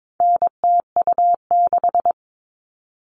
Morse code, heard as a keyed tone, about 22 wpm, 700 Hz. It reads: NTU6
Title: All Cut Numbers - 4-Digit 22wpm